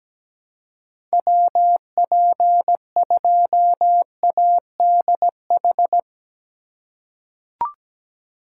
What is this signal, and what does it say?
Morse code, heard as a keyed tone, about 17 wpm, 700 Hz. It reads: WP2ADH
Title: USA Call Signs - 4x Character Spacing - 17wpm